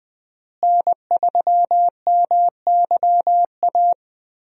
Morse code, heard as a keyed tone, about 21 wpm, 700 Hz. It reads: N3MYA
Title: USA Call Signs - 5 Characters - Repeated 2x - 20wpm